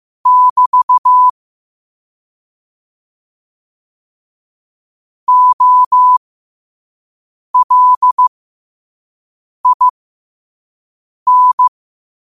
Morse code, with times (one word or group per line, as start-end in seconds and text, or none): 0.3-1.3 <BT>
5.3-11.7 OLIN